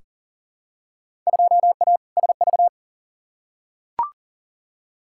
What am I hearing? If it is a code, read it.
Morse code, heard as a keyed tone, about 40 wpm, 700 Hz. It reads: 2A SV
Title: Field Day Exchanges - 40wpm